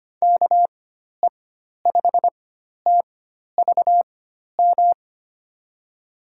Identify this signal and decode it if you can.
Morse code, heard as a keyed tone, about 25 wpm, spaced as for 12 wpm, 700 Hz. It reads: KE5TVM
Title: USA Call Signs - 4x Character Spacing - 25wpm